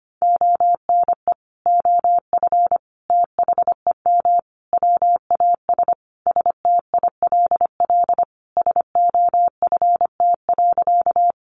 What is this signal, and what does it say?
Morse code, heard as a keyed tone, about 25 wpm, 700 Hz. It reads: ONE OF THEM WAS STILL SOFT.
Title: Sentences from Top 1500 Words 25wpm